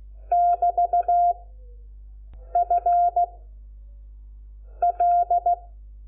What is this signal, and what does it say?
Morse code, heard as a keyed tone, about 16 wpm, 700 Hz. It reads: <BT> F L
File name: Lecciones / CLASE Nº5